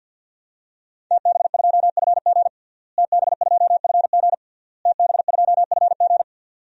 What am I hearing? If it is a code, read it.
Morse code, heard as a keyed tone, about 50 wpm, 700 Hz. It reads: TB2FG TB2FG TB2FG
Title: DXCC-100 Call Signs - 5 Characters - Repeated 3x - 50wpm